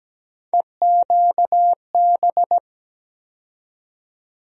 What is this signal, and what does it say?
Morse code, heard as a keyed tone, about 17 wpm, 700 Hz. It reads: EQB
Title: USA Call Sign Postfixes - 3 Letters 17wpm